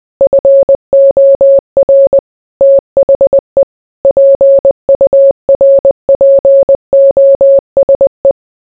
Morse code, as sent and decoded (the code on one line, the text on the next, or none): ..-. --- .-. / - .... / .--. ..- .-. .--. --- ... .
FOR THE PURPOSE